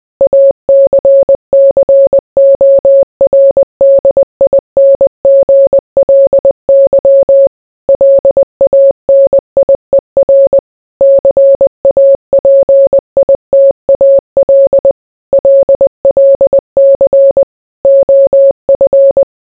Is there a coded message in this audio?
Yes: Morse code from 0.2 to 19.2 s, heard as a keyed tone, about 20 wpm, 550 Hz. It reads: ACCORDINGLY LANIER CAPITAL LLC OF